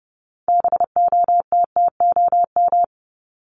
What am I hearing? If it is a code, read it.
Morse code, heard as a keyed tone, about 30 wpm, 700 Hz. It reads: BOTTOM
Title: Top 751-1000 Words 30wpm